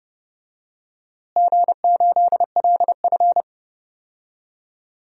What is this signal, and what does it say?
Morse code, heard as a keyed tone, about 30 wpm, 700 Hz. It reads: G8LF
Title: DXCC-100 Call Signs - 4 Characters - Repeated 3x - 30wpm